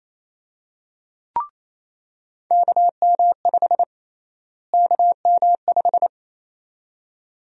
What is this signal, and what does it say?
Morse code, heard as a keyed tone, about 28 wpm, 700 Hz. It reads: KM5 KM5
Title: USA Call Sign Prefixes - Repeated 2x - 28wpm